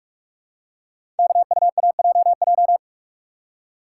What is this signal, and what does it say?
Morse code, heard as a keyed tone, about 44 wpm, 700 Hz. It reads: KUAJJ
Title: Code Groups - 5 Letters - 45wpm